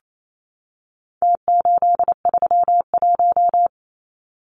Morse code, heard as a keyed tone, about 28 wpm, 700 Hz. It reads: T831
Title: Numbers - Contest Style - 4-Digit 28wpm